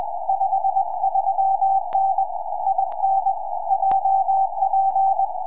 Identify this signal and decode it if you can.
Morse code, heard as a keyed tone, about 21 wpm, 780 Hz. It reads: 53N IN JP
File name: tropo / sk4ao 2m